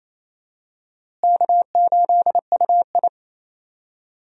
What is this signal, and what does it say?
Morse code, heard as a keyed tone, about 28 wpm, 700 Hz. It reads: K8UI
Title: USA Call Signs - 4 Characters - Repeated 3x - 28wpm